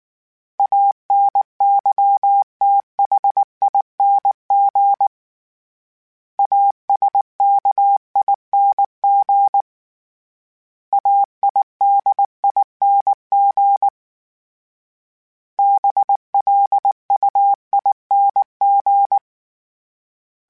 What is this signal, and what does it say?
Morse code, heard as a keyed tone, about 19 wpm, 800 Hz. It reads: ANYTHING ASKING AIDING BLUING
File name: Special INGs / Special INGs-mp3s / Sngl Ing @18